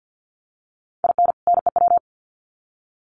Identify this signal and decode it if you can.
Morse code, heard as a keyed tone, about 50 wpm, 700 Hz. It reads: IN NEW
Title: Sets of 2 Words 50wpm